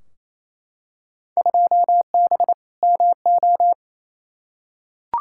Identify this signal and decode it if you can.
Morse code, heard as a keyed tone, about 28 wpm, 700 Hz. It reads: 2B MO
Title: Field Day Exchanges Encore - 28wpm